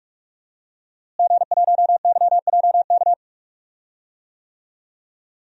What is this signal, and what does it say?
Morse code, heard as a keyed tone, about 42 wpm, 700 Hz. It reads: G1YJK
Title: DXCC-100 Call Signs - 5 Characters - Repeated 2x - 45wpm